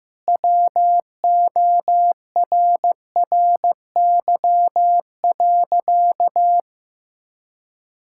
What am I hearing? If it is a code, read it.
Morse code, heard as a keyed tone, about 15 wpm, 700 Hz. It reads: WORRY.
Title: Pangram Challenge v2 - 15wpm